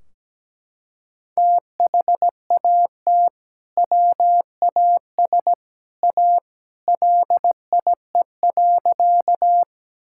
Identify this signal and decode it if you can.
Morse code, heard as a keyed tone, about 17 wpm, 700 Hz. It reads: THAT WAS A LIE.
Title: Sentences from Top 1600 Words 17wpm